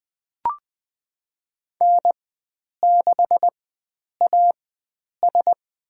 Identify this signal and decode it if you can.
Morse code, heard as a keyed tone, about 20 wpm, spaced as for 9 wpm, 700 Hz. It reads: N6AS
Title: USA Call Signs - 4 Characters - 4x Character Spacing - 20wpm